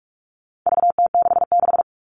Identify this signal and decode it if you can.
Morse code, heard as a keyed tone, about 44 wpm, 700 Hz. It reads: VT66
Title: All Cut Numbers - 4-Digit 45wpm